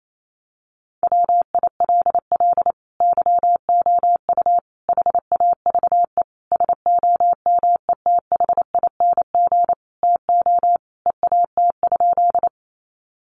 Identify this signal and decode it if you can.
Morse code, heard as a keyed tone, about 28 wpm, 700 Hz. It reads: WILL YOU HAVE SOMETHING TO EAT?
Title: Sentences from Top 600 Words 28wpm